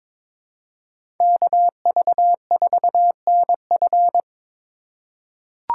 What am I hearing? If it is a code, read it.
Morse code, heard as a keyed tone, about 23 wpm, 700 Hz. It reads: KV4NF
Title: USA Call Signs - 5 Characters - 4x Character Spacing - 22wpm